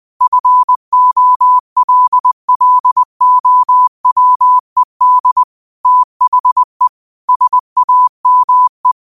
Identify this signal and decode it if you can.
Morse code, heard as a keyed tone, about 20 wpm, 1000 Hz. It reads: FOLLOWED THE SAME
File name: qotdfiles / 20wpm